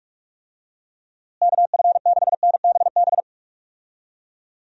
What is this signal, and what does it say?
Morse code, heard as a keyed tone, about 44 wpm, 700 Hz. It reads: KF6NBB
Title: USA Call Signs - 6 Characters - Repeated 3x - 45wpm